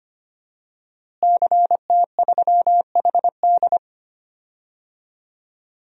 Morse code, heard as a keyed tone, about 25 wpm, 700 Hz. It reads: CT3HD
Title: DXCC-100 Call Signs - 5 Characters - Repeated 2x - 25wpm